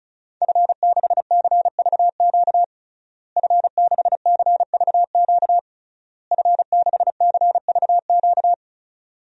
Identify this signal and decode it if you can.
Morse code, heard as a keyed tone, about 36 wpm, 700 Hz. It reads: F6CVQ F6CVQ F6CVQ
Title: DXCC-100 Call Signs - 5 Characters - Repeated 3x - 35wpm